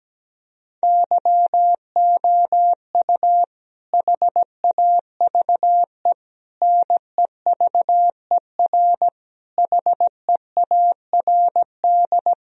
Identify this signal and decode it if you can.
Morse code, heard as a keyed tone, about 17 wpm, 700 Hz. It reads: YOU HAVE NEVER HEARD
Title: Sentences from Top 1000 Words - Review Part 2 - 17wpm